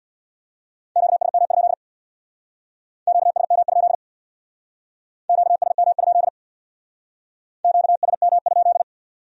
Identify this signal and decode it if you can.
Morse code, heard as a keyed tone, about 61 wpm, 700 Hz. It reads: QSM? QSM? QSM? QSM?
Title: Speed-Racing Q-Codes - 50wpm